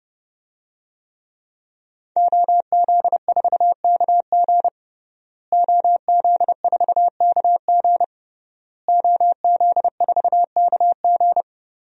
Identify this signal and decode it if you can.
Morse code, heard as a keyed tone, about 30 wpm, 700 Hz. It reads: OZ4KG OZ4KG OZ4KG
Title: DXCC-100 Call Signs - 5 Characters - Repeated 3x - 30wpm